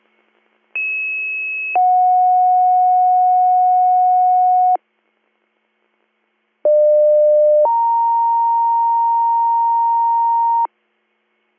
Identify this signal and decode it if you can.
Two-tone page, heard as a one-second tone, then a three-second tone, about 2600 then 730 Hz, about 600 then 930 Hz. It is heard from 0.7 to 10.7 s.